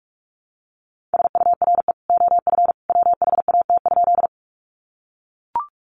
Morse code, heard as a keyed tone, about 44 wpm, 700 Hz. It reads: SURE OF WHAT?
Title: Sentences from Top 300 Words 45wpm